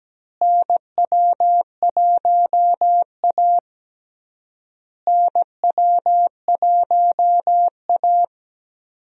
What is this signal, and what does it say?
Morse code, heard as a keyed tone, about 17 wpm, 700 Hz. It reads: NW1A NW1A
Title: USA Call Signs - 4 Characters - Repeated 3x - 17wpm